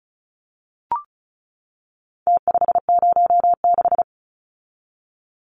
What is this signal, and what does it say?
Morse code, heard as a keyed tone, about 34 wpm, 700 Hz. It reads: T506
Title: Numbers - Contest Style - 4-Digit 35wpm